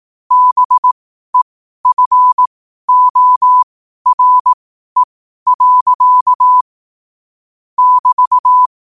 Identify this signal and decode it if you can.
Morse code, heard as a keyed tone, about 18 wpm, spaced as for 12 wpm, 1000 Hz. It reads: BEFORE. <BT>
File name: qotdfiles / 12wpm